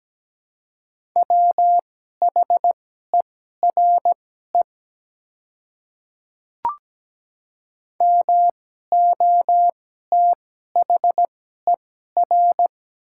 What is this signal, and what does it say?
Morse code, heard as a keyed tone, about 17 wpm, spaced as for 12 wpm, 700 Hz. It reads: WHERE MOTHER
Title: 2x Character Spacing - Top 200 Words - 17wpm